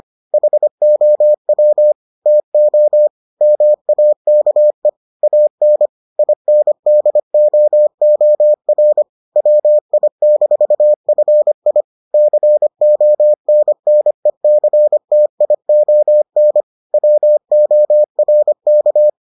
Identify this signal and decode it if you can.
Morse code, heard as a keyed tone, about 25 wpm, 600 Hz. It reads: HOW TO MAKE AN INDOOR WI-FI CONNECTION WORK